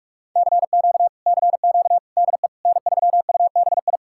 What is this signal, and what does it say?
Morse code, heard as a keyed tone, about 67 wpm, spaced as for 56 wpm, 700 Hz. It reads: CQ CQ DE N3UBI
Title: Speed-Racing Common QSO Phrases - 45wpm